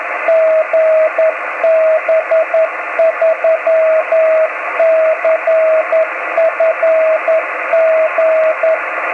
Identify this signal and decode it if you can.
Morse code, heard as a keyed tone, about 11 wpm, 650 Hz, with radio noise. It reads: GB3CFG
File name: om3pv 70mhz / zaznam / gb3cfg